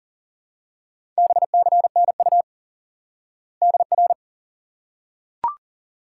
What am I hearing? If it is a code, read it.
Morse code, heard as a keyed tone, about 38 wpm, 700 Hz. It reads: BCNU DR
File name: IQE Course 3 - Lesson 89 - YAGI - Two Elements - 40wpm